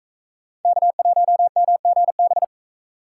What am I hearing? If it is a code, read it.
Morse code, heard as a keyed tone, about 42 wpm, 700 Hz. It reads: K1MGB